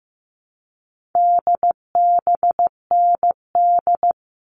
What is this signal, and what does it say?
Morse code, heard as a keyed tone, about 15 wpm, 700 Hz. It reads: DBND
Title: All Cut Numbers - 4-Digit 15wpm